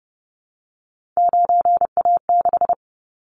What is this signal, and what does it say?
Morse code, heard as a keyed tone, about 30 wpm, 700 Hz. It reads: 9A6
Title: Callsign Prefixes for Top 100 DXCC - 30wpm